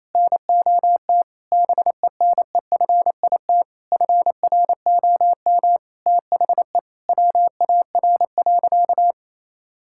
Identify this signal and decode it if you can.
Morse code, heard as a keyed tone, about 28 wpm, 700 Hz. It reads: NOT BENEFIT FROM THE WAR.
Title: Sentences from Top 1000 Words - Review Part 6 - 28wpm